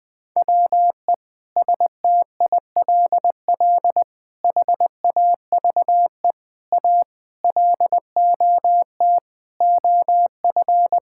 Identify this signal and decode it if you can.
Morse code, heard as a keyed tone, about 20 wpm, 700 Hz. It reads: WE STILL HAVE A LOT OF
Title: Sets of 9 Words - 20wpm